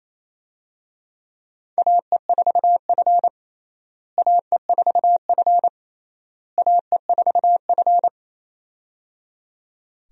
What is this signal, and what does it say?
Morse code, heard as a keyed tone, about 28 wpm, 700 Hz. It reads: AE4F AE4F AE4F
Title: USA Call Signs - 4 Characters - Repeated 3x - 28wpm